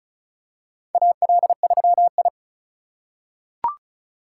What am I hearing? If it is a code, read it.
Morse code, heard as a keyed tone, about 34 wpm, 700 Hz. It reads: AL3I
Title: USA Call Signs - 4 Characters - 3x Character Spacing - 35wpm